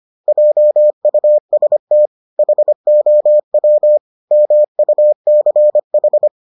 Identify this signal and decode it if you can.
Morse code, heard as a keyed tone, about 25 wpm, 600 Hz. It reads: JUST HOW MUCH